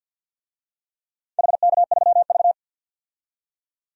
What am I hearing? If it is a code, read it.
Morse code, heard as a keyed tone, about 52 wpm, 700 Hz. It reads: HK3V
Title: DXCC-100 Call Signs - 4 Characters - 50wpm